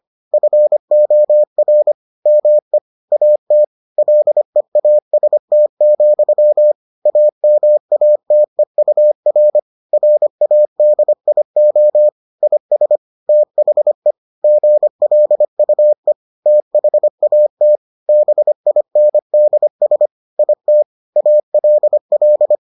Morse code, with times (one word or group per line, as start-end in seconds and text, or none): 0.3-1.9 FOR
2.3-2.8 ME
3.1-3.6 AT
4.0-6.7 LEAST,
7.1-9.6 AMATEUR
9.9-12.1 RADIO
12.4-13.0 IS
13.3-14.1 THE
14.4-16.1 GLUE
16.5-17.8 THAT
18.1-20.1 BINDS
20.4-20.8 IT
21.2-22.6 ALL